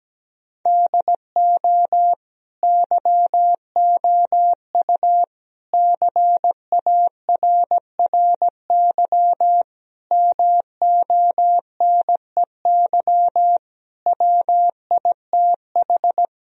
Morse code, heard as a keyed tone, about 17 wpm, 700 Hz. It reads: DO YOU CARRY MONEY WITH
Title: Sentences from Top 1000 Words - Review Part 1 - 17wpm